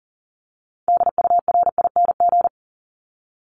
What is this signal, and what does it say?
Morse code, heard as a keyed tone, about 40 wpm, 700 Hz. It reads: DURING
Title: Top 500 Words 40wpm